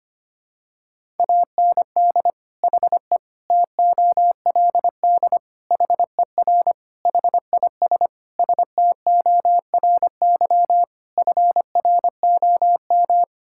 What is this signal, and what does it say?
Morse code, heard as a keyed tone, about 25 wpm, 700 Hz. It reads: AND HE TOLD HER HIS STORY FROM
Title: Sentences from Top 1000 Words - Review Part 5 - 25wpm